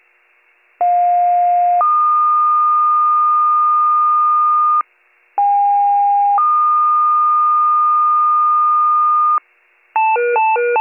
Two-tone page, heard as a one-second tone, then a three-second tone, about 710 then 1220 Hz, about 810 then 1220 Hz. It is heard from 0.8 to 9.4 s.